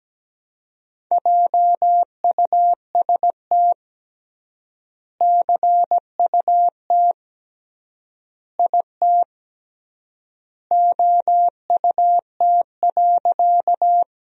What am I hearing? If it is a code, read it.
Morse code, heard as a keyed tone, about 17 wpm, 700 Hz. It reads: JUST CUT IT OUT.
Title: 3x Word-Spacing - Sentences from Top 300 Words - 17wpm